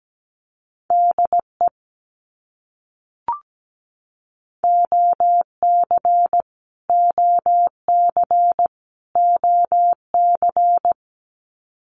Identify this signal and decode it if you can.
Morse code, heard as a keyed tone, about 17 wpm, 700 Hz. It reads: DE OC OC OC
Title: Abbreviations Repeated 3x - 17wpm